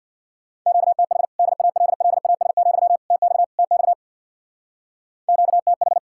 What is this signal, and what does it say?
Morse code, heard as a keyed tone, about 60 wpm, 700 Hz. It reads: QTH DALLAS, TX TX QTH
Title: Speed-Racing Common QSO Phrases Encore - 50wpm